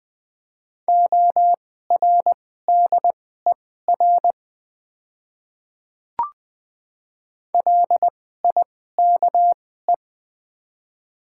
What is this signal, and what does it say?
Morse code, heard as a keyed tone, about 20 wpm, spaced as for 14 wpm, 700 Hz. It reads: ORDER LIKE
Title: 2x Character Spacing - Top 500 Words - 20wpm